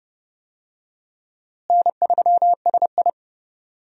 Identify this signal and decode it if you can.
Morse code, heard as a keyed tone, about 30 wpm, 700 Hz. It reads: N3SI
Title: USA Call Signs - 4 Characters - Repeated 2x - 30wpm